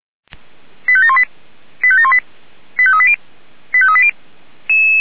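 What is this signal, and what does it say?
Five-tone selective call (ZVEI), heard as quick runs of five tones, tones of about 70 ms. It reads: 85718 85718 85380 85380
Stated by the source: ZVEI five-tone call received by SDR, from the feuerwehr-orlamuende.org fire brigade server.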